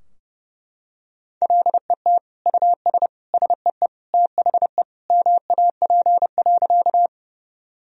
Morse code, heard as a keyed tone, about 30 wpm, 700 Hz. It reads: LET US SEE THE MAP.
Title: Sentences from Top 1000 Words - Review Part 2 - 30wpm